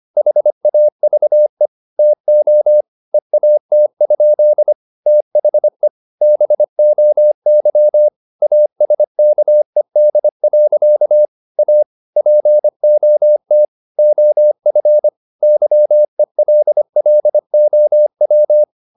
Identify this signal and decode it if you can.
Morse code, heard as a keyed tone, about 25 wpm, 600 Hz. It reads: HAVE TO EAT? THE BOY ASKED. A POT OF YELLOW
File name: the old man and the sea part 1 25wpm